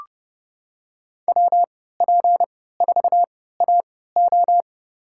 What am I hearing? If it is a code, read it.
Morse code, heard as a keyed tone, about 30 wpm, spaced as for 17 wpm, 700 Hz. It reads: WP4AO
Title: USA Call Signs - 5 Characters - 3x Character Spacing - 30wpm